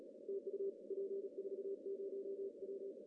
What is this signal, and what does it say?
Morse code, heard as a keyed tone, about 33 wpm, 390 Hz. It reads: TU P40W
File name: CW / cqww-cw-k5tr